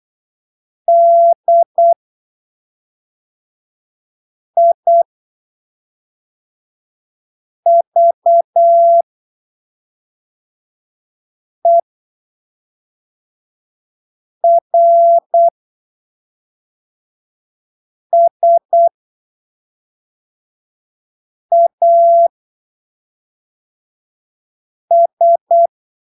Morse code, heard as a keyed tone, about 8 wpm, spaced as for 3 wpm, 680 Hz. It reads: DIVERSAS
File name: arquivos / texto / texto 06 08ppm